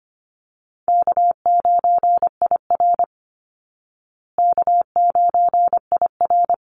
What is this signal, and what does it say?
Morse code, heard as a keyed tone, about 25 wpm, 700 Hz. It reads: K9IR K9IR
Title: Call Signs Repeated 2x - 25wpm